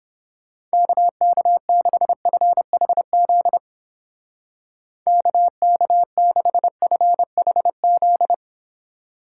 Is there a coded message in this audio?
Yes: Morse code from 0.7 to 8.3 s, heard as a keyed tone, about 31 wpm, 700 Hz. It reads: KK6FHZ KK6FHZ